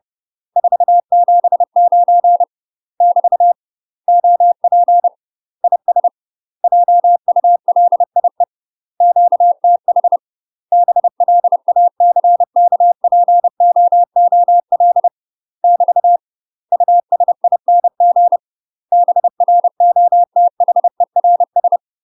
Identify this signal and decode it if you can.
Morse code, heard as a keyed tone, about 30 wpm, 700 Hz. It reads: 479 <BT> OP IS JULIE QTH BLACKPOOL <BT> USING BROTHERS